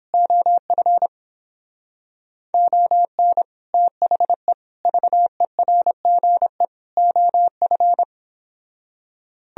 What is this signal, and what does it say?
Morse code, heard as a keyed tone, about 31 wpm, 700 Hz. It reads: OF ON THE VERGE OF